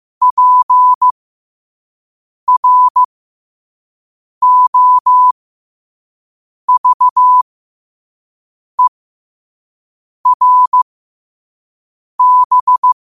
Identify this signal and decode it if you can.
Morse code, heard as a keyed tone, about 15 wpm, spaced as for 5 wpm, 1000 Hz. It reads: PROVERB
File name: qotdfiles / 5wpm